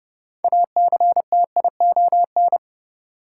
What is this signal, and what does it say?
Morse code, heard as a keyed tone, about 30 wpm, 700 Hz. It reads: ACTION
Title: Sentences from Top 700 Words - New Word Drill - 30wpm